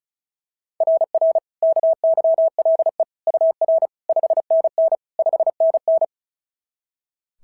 Morse code, heard as a keyed tone, about 36 wpm, 660 Hz. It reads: RR KYLE UR 5NN 5NN
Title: SOTA QSO Practice 35wpm